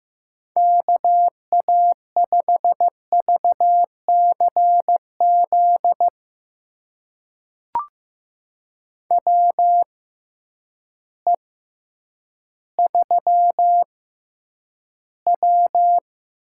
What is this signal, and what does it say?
Morse code, heard as a keyed tone, about 15 wpm, spaced as for 9 wpm, 700 Hz. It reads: KA5VCZ WE3W
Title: USA Call Signs - 6x Character Spacing - 15wpm